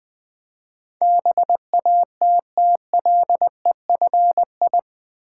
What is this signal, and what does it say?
Morse code, heard as a keyed tone, about 21 wpm, 700 Hz. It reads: BATTLEFI
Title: Build A Word - 4 Letters At A Time - 11-Letter Words - 20wpm